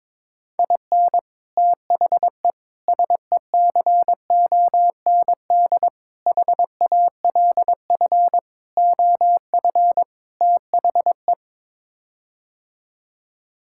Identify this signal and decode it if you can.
Morse code, heard as a keyed tone, about 22 wpm, 700 Hz. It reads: IN THE SECOND HALF OF THE
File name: practice / 22wpm / Sets of 6 Words - 22wpm